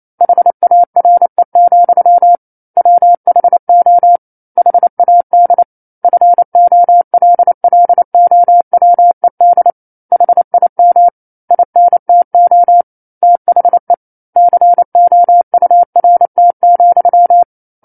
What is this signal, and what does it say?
Morse code, heard as a keyed tone, about 28 wpm, 700 Hz. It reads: HARE, WHO HAD FOLLOWED HIM INTO THE COURT,
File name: 09-Chapter-09 28 WPM